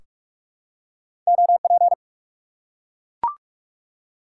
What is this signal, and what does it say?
Morse code, heard as a keyed tone, about 44 wpm, 700 Hz. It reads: OP
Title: Top 90 QSO Elements 45wpm